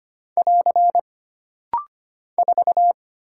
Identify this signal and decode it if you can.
Morse code, heard as a keyed tone, about 25 wpm, 700 Hz. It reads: <AR> 4
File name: Single Character (Letter:Number:Punctuation:Prosign) - Rapid-Fire - 25wpm